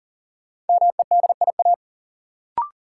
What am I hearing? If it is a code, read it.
Morse code, heard as a keyed tone, about 40 wpm, 700 Hz. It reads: MEDIA
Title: Sentences from Top 1000 Words - Review Part 4 - New Word Drill - 40wpm